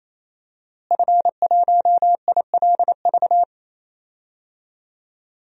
Morse code, heard as a keyed tone, about 28 wpm, 700 Hz. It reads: F1ILV
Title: DXCC-100 Call Signs - 5 Characters - Repeated 3x - 28wpm